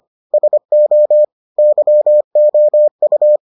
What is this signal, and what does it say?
Morse code, heard as a keyed tone, about 25 wpm, 600 Hz. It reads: SO YOU